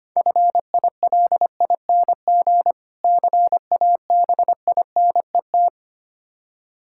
Morse code, heard as a keyed tone, about 25 wpm, 700 Hz. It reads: FILING CABINET
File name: practice / 25wpm / Difficult Words - Speed-Racing with Hints - 25wpm